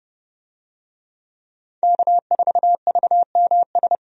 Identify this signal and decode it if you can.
Morse code, heard as a keyed tone, about 30 wpm, 700 Hz. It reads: K4VMS
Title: USA Call Signs - 5 Characters - 30wpm